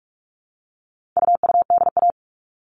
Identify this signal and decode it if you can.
Morse code, heard as a keyed tone, about 44 wpm, 700 Hz. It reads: UUDA
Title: All Cut Numbers - 4-Digit 45wpm